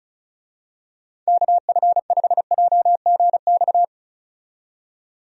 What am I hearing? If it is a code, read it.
Morse code, heard as a keyed tone, about 36 wpm, 700 Hz. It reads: KF5JGX
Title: USA Call Signs - 6 Characters - 35wpm